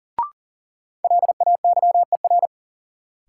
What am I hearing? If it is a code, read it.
Morse code, heard as a keyed tone, about 40 wpm, 700 Hz. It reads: LAYER
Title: Sentences from Top 1600 Words - New Word Drill - 40wpm